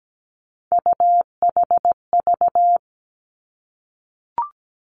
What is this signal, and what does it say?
Morse code, heard as a keyed tone, about 17 wpm, 700 Hz. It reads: UHV
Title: Three Challenging Letter-Number Combinations 17wpm